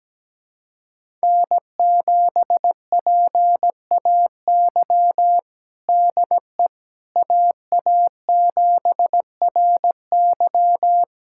Morse code, heard as a keyed tone, about 17 wpm, 700 Hz. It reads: N7PAY DE AA7RY
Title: USA Callsign DE Callsign - 5 Characters - 17wpm